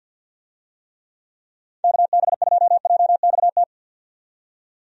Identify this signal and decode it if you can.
Morse code, heard as a keyed tone, about 50 wpm, 700 Hz. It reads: KB2JXT
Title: USA Call Signs - 6 Characters - 50wpm